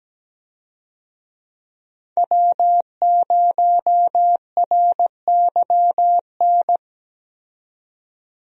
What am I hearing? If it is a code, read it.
Morse code, heard as a keyed tone, about 17 wpm, 700 Hz. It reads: W0RYN
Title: USA Call Signs - 5 Characters - 17wpm